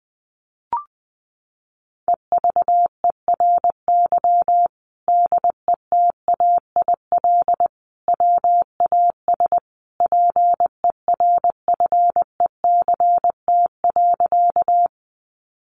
Morse code, heard as a keyed tone, about 20 wpm, 700 Hz. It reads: EVERY DETAIL WAS PERFECT.
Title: Sentences from Top 1500 Words 20wpm